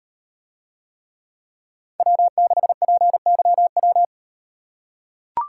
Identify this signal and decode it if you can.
Morse code, heard as a keyed tone, about 36 wpm, 700 Hz. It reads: W6PYW